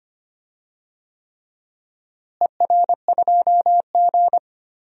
Morse code, heard as a keyed tone, about 25 wpm, 700 Hz. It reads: ER2G
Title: DXCC-100 Call Signs - 4 Characters - Repeated 3x - 25wpm